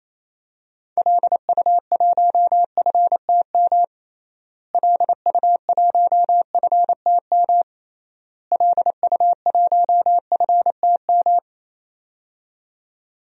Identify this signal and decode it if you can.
Morse code, heard as a keyed tone, about 28 wpm, 700 Hz. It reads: LU1FTM LU1FTM LU1FTM
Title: DXCC-100 Call Signs - 6 Characters - Repeated 3x - 28wpm